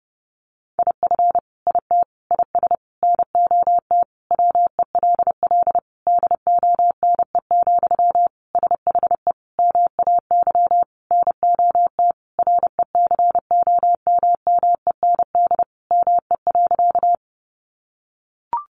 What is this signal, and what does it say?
Morse code, heard as a keyed tone, about 30 wpm, 700 Hz. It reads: IF IT IS NOT WELL DONE, SHE MAY NOT RECOMMEND ME.
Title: Sentences from Top 1200 Words 30wpm